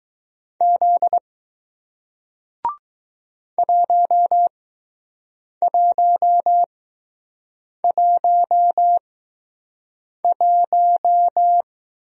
Morse code, heard as a keyed tone, about 20 wpm, spaced as for 6 wpm, 700 Hz. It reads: Z 1111